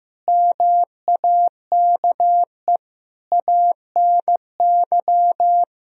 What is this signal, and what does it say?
Morse code, heard as a keyed tone, about 16 wpm, 700 Hz. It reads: MAKE ANY
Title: Sentences from Top 1000 Words - Review Part 4 - 15wpm